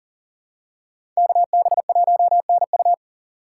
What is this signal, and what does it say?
Morse code, heard as a keyed tone, about 40 wpm, 690 Hz. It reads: KB1NU
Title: USA Call Signs - 3x Character Spacing - 40wpm